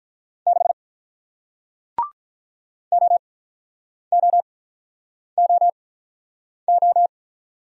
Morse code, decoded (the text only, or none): 6 OOOO